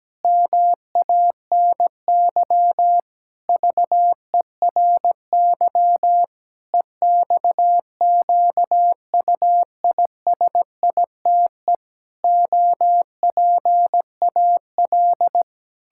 Morse code, heard as a keyed tone, about 17 wpm, 700 Hz. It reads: MANY VERY EXQUISITE OPAL
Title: Pangram Challenge v2 - 17wpm